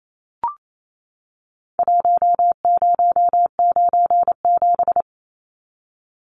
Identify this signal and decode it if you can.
Morse code, heard as a keyed tone, about 28 wpm, 700 Hz. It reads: 1097
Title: Numbers - Contest Style - 4-Digit 28wpm